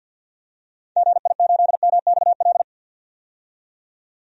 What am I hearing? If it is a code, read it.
Morse code, heard as a keyed tone, about 51 wpm, 700 Hz. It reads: GI8MXL
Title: DXCC-100 Call Signs - 6 Characters - 50wpm